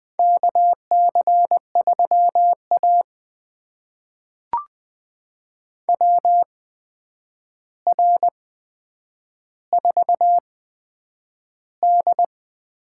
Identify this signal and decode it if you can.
Morse code, heard as a keyed tone, about 20 wpm, spaced as for 9 wpm, 700 Hz. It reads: KC3A WR4D
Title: USA Call Signs - 4 Characters - 8x Character Spacing - 20wpm